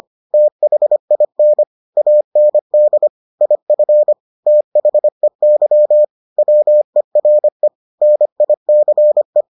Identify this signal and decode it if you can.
Morse code, heard as a keyed tone, about 25 wpm, 600 Hz. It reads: THIN AND IF THEY WERE NICE